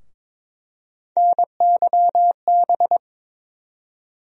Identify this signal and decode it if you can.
Morse code, heard as a keyed tone, about 21 wpm, 700 Hz. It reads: NYB
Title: USA Call Sign Postfixes - 3 Letters 22wpm